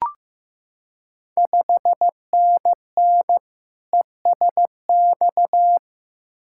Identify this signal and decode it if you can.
Morse code, heard as a keyed tone, about 15 wpm, 700 Hz. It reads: 5NN ESX
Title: Ontario QSO Party Exchange 15wpm